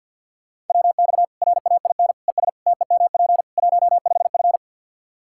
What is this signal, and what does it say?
Morse code, heard as a keyed tone, about 48 wpm, 700 Hz. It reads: WX RAIN ES TEMP 15F
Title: Practice QSO - Exchange 2 - Info - 50wpm